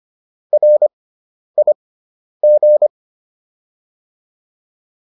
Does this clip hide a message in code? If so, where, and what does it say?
Yes: Morse code from 0.5 to 2.9 s, heard as a keyed tone, about 25 wpm, spaced as for 10 wpm, 600 Hz. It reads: RIG